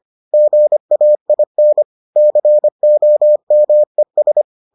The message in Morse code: GAIN COMES

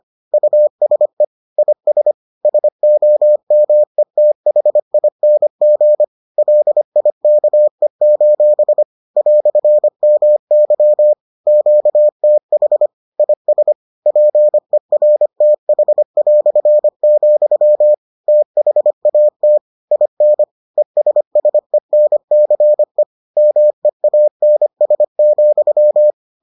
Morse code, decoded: USE IS SOMETHING LIKE: "MY QTH IS PERTH", THAT IN ESSENCE MEANS,

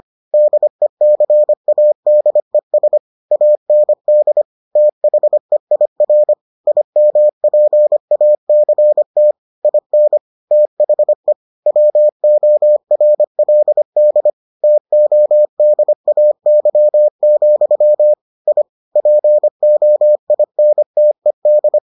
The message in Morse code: DECADES AND THEIR IMPACT IN THE WORLD TODAY, I POINTED

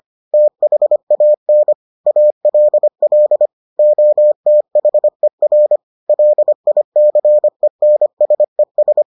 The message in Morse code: THAN ALL OTHER LICENSES

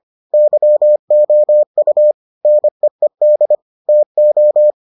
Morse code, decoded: YOU NEED TO

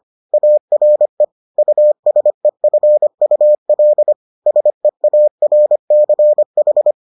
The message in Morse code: ARE USEFUL SEARCH

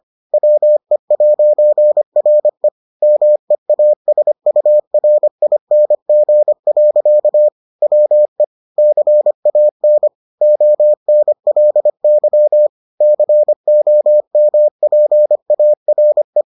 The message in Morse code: WE'RE MEASURING. WE CAN ONLY COMPARE

